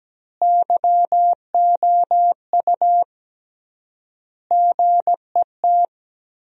Morse code, heard as a keyed tone, about 17 wpm, 700 Hz. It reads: YOU GET
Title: 3x Word-Spacing - Sentences from Top 200 Words - 17wpm